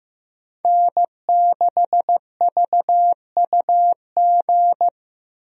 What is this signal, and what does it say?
Morse code, heard as a keyed tone, about 15 wpm, 700 Hz. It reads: N6VUG
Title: USA Call Signs - 5 Characters - Repeated 2x - 15wpm